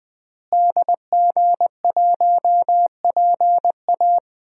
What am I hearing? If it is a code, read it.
Morse code, heard as a keyed tone, about 21 wpm, 700 Hz. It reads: DG1PA
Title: DXCC-100 Call Signs - 5 Characters - Repeated 2x - 20wpm